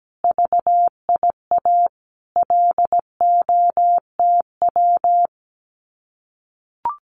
Morse code, heard as a keyed tone, about 17 wpm, 700 Hz. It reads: VIA LOTW
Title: Common QSO Phrases Encore 17wpm